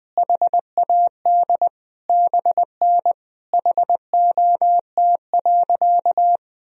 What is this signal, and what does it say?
Morse code, heard as a keyed tone, about 20 wpm, 700 Hz. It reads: HAD BN HOT.
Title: Sentences from Top 500 Words and Abbreviations 20wpm